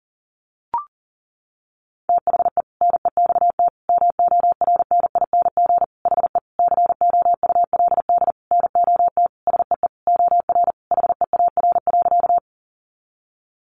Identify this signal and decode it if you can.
Morse code, heard as a keyed tone, about 37 wpm, 700 Hz. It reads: THE NEXT MORNING HE COULD NOT SEE OR HEAR.
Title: Sentences from Top 600 Words 40wpm